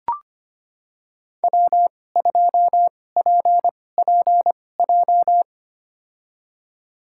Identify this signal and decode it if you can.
Morse code, heard as a keyed tone, about 25 wpm, spaced as for 18 wpm, 700 Hz. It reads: W2PPJ
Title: USA Call Signs - 5 Characters - 2x Character Spacing - 25wpm